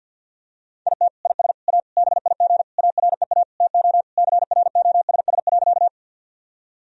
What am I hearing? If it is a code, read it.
Morse code, heard as a keyed tone, about 51 wpm, 700 Hz. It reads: IT IS A BIG AREA TO CROSS.
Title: Sentences from Top 1000 Words - Review Part 2 - 50wpm